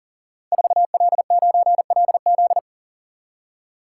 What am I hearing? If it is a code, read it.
Morse code, heard as a keyed tone, about 38 wpm, 700 Hz. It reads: 4L9LZ
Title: Code Groups - 5 Letter:Numbers - 40wpm